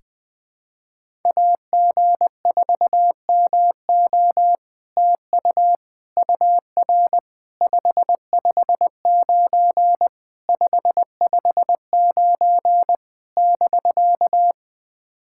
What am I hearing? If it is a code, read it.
Morse code, heard as a keyed tone, about 20 wpm, 700 Hz. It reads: AG4MO TU UR 559 559 <BK>
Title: SOTA QSO Practice 20wpm